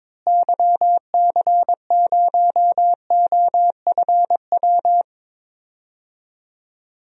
Morse code, heard as a keyed tone, about 22 wpm, 700 Hz. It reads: YC0OFW
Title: DXCC-100 Call Signs - 6 Characters - Repeated 3x - 22wpm